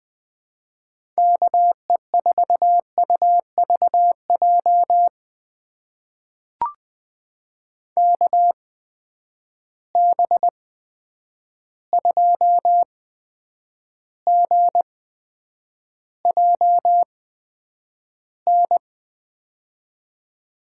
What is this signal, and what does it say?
Morse code, heard as a keyed tone, about 20 wpm, spaced as for 9 wpm, 700 Hz. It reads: KE4UVJ KB2GJN
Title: USA Call Signs - 6 Characters - 8x Character Spacing - 20wpm